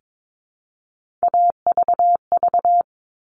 Morse code, heard as a keyed tone, about 22 wpm, 700 Hz. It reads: AVV
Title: All Cut Numbers - 3-Digit 22wpm